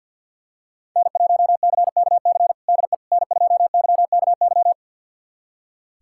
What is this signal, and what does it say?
Morse code, heard as a keyed tone, about 47 wpm, 700 Hz. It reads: N1CKC DE N2YKY
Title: USA Callsign DE Callsign - 5 Characters - 50wpm